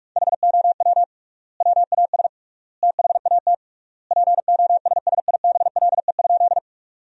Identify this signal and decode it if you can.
Morse code, heard as a keyed tone, about 46 wpm, 700 Hz. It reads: HOW WAS THAT POSSIBLE?
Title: 3x Word-Spacing - Sentences from Top 500 Words - 45wpm